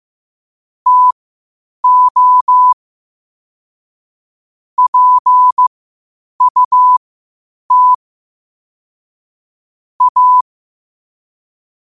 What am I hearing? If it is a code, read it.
Morse code, heard as a keyed tone, about 15 wpm, spaced as for 8 wpm, 1000 Hz. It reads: TO PUT A
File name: qotdfiles / 8wpm